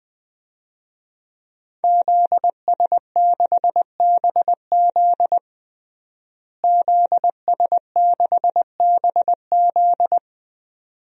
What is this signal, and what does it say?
Morse code, heard as a keyed tone, about 20 wpm, 700 Hz. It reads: ZS6BZ ZS6BZ
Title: DXCC-100 Call Signs - 5 Characters - Repeated 2x - 20wpm